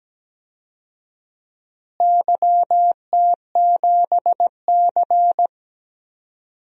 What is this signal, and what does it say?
Morse code, heard as a keyed tone, about 17 wpm, 700 Hz. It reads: YT7C
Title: DXCC-100 Call Signs - 4 Characters - Repeated 3x - 17wpm